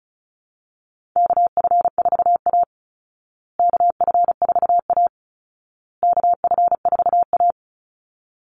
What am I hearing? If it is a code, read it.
Morse code, heard as a keyed tone, about 34 wpm, 700 Hz. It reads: KF4A KF4A KF4A
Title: Call Signs Repeated 2x - 35wpm